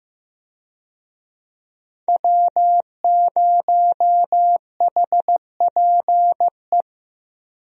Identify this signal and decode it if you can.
Morse code, heard as a keyed tone, about 15 wpm, 700 Hz. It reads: W0HPE
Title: USA Call Signs - 5 Characters - Repeated 3x - 15wpm